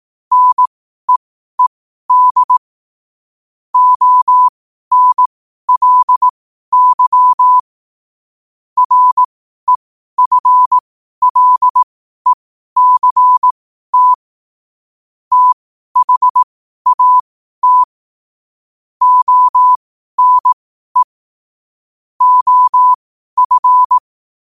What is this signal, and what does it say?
Morse code, heard as a keyed tone, about 18 wpm, spaced as for 12 wpm, 1000 Hz. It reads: NEED ONLY REFLECT THAT ONE OF
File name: qotdfiles / 12wpm